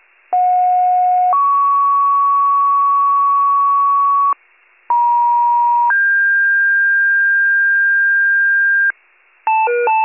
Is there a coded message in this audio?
Yes: two-tone page from 0.3 to 8.9 s, heard as a one-second tone, then a three-second tone, about 730 then 1120 Hz, about 950 then 1600 Hz, with noise.